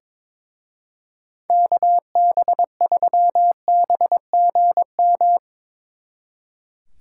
Morse code, heard as a keyed tone, about 25 wpm, 700 Hz. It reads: KB3BGM
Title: Speed Racing - USA 2x3 Call Signs 22wpm